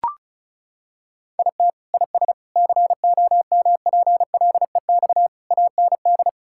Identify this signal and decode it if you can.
Morse code, heard as a keyed tone, about 36 wpm, 700 Hz. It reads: IT IS COMPLEX AND
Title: Sentences from Top 1000 Words - Review Part 6 - 35wpm